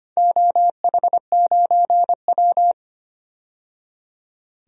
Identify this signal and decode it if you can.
Morse code, heard as a keyed tone, about 25 wpm, 700 Hz. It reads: OH9W
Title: DXCC-100 Call Signs - 4 Characters - 25wpm